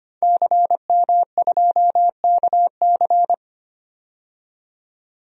Morse code, heard as a keyed tone, about 25 wpm, 700 Hz. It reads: CM2KC
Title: DXCC-100 Call Signs - 5 Characters - 25wpm